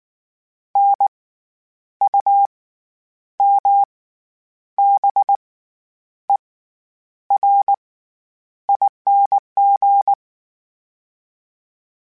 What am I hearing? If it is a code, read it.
Morse code, heard as a keyed tone, about 19 wpm, spaced as for 9 wpm, 800 Hz. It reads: NUMBERING
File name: Special INGs / Special INGs-mp3s / SlowToFaster ing @18